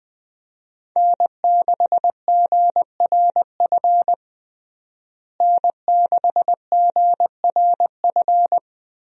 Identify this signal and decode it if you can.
Morse code, heard as a keyed tone, about 21 wpm, 700 Hz. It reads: N6GRF N6GRF
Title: USA Call Signs - 5 Characters - Repeated 3x - 20wpm